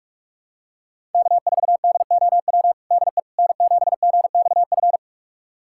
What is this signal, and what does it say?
Morse code, heard as a keyed tone, about 46 wpm, 700 Hz. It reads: K4DOW DE N7GXF
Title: USA Callsign DE Callsign - 5 Characters - 45wpm